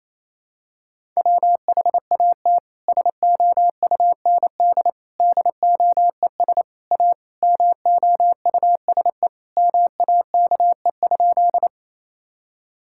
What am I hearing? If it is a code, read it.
Morse code, heard as a keyed tone, about 27 wpm, 700 Hz. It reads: WHAT SOUND DOES A MOUSE MAKE?
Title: Sentences from Top 1000 Words - Review Part 5 - 28wpm